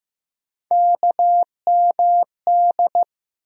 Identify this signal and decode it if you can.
Morse code, heard as a keyed tone, about 15 wpm, 700 Hz. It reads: KMD
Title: USA Call Sign Postfixes - 3 Letters 15wpm